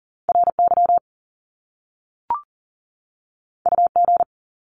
Morse code, heard as a keyed tone, about 40 wpm, 700 Hz. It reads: RY UG
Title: Top Two Letter Combinations in Words 40wpm